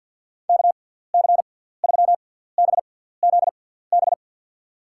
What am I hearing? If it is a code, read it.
Morse code, heard as a keyed tone, about 52 wpm, 700 Hz. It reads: K C 3 B Z B
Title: USA Call Signs - 6 Characters - 6x Character Spacing - 50wpm